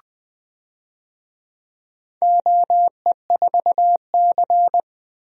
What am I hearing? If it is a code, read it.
Morse code, heard as a keyed tone, about 20 wpm, 700 Hz. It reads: OE4C
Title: DXCC-100 Call Signs - 4 Characters - 20wpm